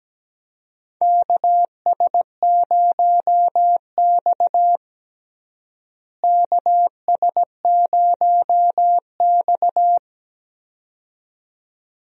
Morse code, heard as a keyed tone, about 17 wpm, 700 Hz. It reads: KS0X KS0X
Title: USA Call Signs - 4 Characters - Repeated 2x - 17wpm